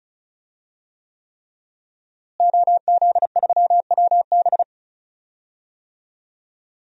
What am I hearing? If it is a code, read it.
Morse code, heard as a keyed tone, about 36 wpm, 700 Hz. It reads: OZ3WB
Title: DXCC-100 Call Signs - 5 Characters - Repeated 3x - 35wpm